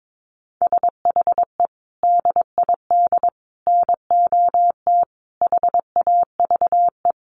Morse code, heard as a keyed tone, about 22 wpm, 700 Hz. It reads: SHE DID NOT HAVE
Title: Sets of 5 Words Encore2 22wpm